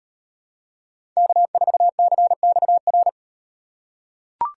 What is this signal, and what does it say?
Morse code, heard as a keyed tone, about 38 wpm, 700 Hz. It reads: K4CXR